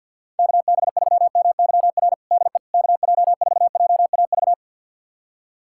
Morse code, heard as a keyed tone, about 48 wpm, 700 Hz. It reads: KB3MYR DE KJ4JAV
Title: USA Callsign DE Callsign - 6 Characters - 50wpm